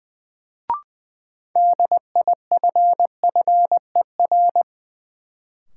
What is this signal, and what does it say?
Morse code, heard as a keyed tone, about 21 wpm, 700 Hz. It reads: DIFFER
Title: Sentences from Top 1100 Words - New Word Drill Alt - 20wpm